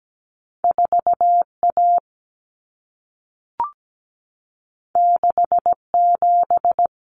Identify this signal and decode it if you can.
Morse code, heard as a keyed tone, about 17 wpm, 700 Hz. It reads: 4A 67
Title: Common Cut Numbers - 2-Digit 17wpm